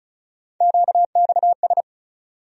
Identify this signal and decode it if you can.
Morse code, heard as a keyed tone, about 36 wpm, 700 Hz. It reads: QXS
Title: USA Call Sign Postfixes - 3 Letters 35wpm